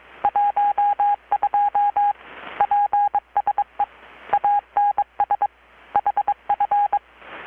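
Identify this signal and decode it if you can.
Morse code, heard as a keyed tone, about 22 wpm, 820 Hz. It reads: 12 PSE ANS HF
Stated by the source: Morse from KPH's Night of Nights XX maritime radiotelegraphy event, received in Chiapas, Mexico about 3,800 km away on a Kenwood R-600 with a grounded 10 m long-wire antenna.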